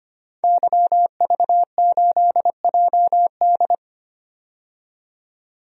Morse code, heard as a keyed tone, about 25 wpm, 700 Hz. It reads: YV8JD
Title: DXCC-100 Call Signs - 5 Characters - 25wpm